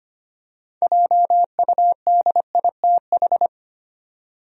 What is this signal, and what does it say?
Morse code, heard as a keyed tone, about 25 wpm, 700 Hz. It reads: JUDITH
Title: USA Common Names - 25wpm